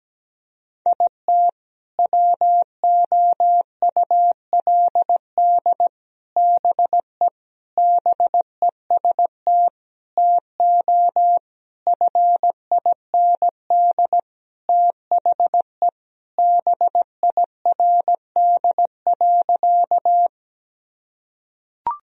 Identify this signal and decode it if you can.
Morse code, heard as a keyed tone, about 17 wpm, 700 Hz. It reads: IT WOULD BE BEST TO FIND THE BIRD.
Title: Sentences from Top 1000 Words - Review Part 2 - 17wpm